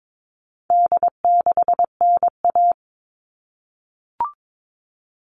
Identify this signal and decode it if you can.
Morse code, heard as a keyed tone, about 22 wpm, 700 Hz. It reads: D6NA
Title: All Cut Numbers - 4-Digit 22wpm